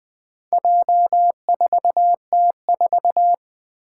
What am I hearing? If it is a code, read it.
Morse code, heard as a keyed tone, about 20 wpm, 700 Hz. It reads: J4T4
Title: Four Challenging Letter-Number Combinations 20wpm